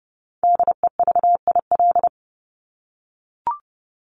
Morse code, heard as a keyed tone, about 30 wpm, 700 Hz. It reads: DEVIL
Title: Top 1751-2000 Words 30wpm